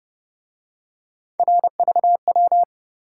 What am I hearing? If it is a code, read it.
Morse code, heard as a keyed tone, about 29 wpm, 700 Hz. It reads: RVW